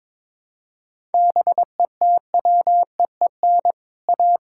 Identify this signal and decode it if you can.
Morse code, heard as a keyed tone, about 22 wpm, 700 Hz. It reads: BETWEEN A
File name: practice / 22wpm / Sets of 7 Words - 22wpm